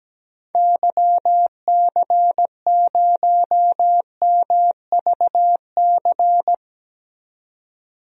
Morse code, heard as a keyed tone, about 17 wpm, 700 Hz. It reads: YC0MVC
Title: DXCC-100 Call Signs - 6 Characters - 17wpm